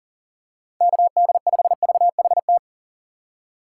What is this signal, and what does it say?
Morse code, heard as a keyed tone, about 40 wpm, 700 Hz. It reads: KD5VHT
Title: USA Call Signs - 6 Characters - 3x Character Spacing - 40wpm